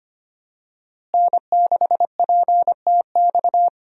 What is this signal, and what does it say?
Morse code, heard as a keyed tone, about 25 wpm, 700 Hz. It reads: N6PTX
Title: USA Call Signs - 5 Characters - Repeated 3x - 25wpm